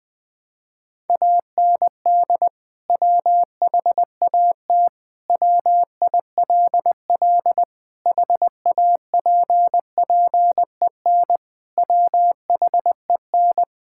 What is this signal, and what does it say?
Morse code, heard as a keyed tone, about 20 wpm, 700 Hz. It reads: AND WHAT WILL HAPPEN WHEN
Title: Sentences from Top 1000 Words - Review Part 5 - 20wpm